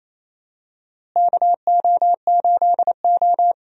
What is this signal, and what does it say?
Morse code, heard as a keyed tone, about 28 wpm, 700 Hz. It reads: KO8O
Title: USA Call Signs - 4 Characters - Repeated 2x - 28wpm